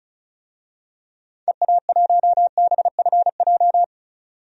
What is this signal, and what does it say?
Morse code, heard as a keyed tone, about 36 wpm, 700 Hz. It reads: EA1BFJ
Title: DXCC-100 Call Signs - 6 Characters - 35wpm